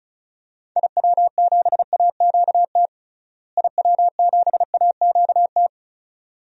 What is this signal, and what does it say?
Morse code, heard as a keyed tone, about 36 wpm, 700 Hz. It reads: IW7AQT IW7AQT
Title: DXCC-100 Call Signs - 6 Characters - Repeated 2x - 35wpm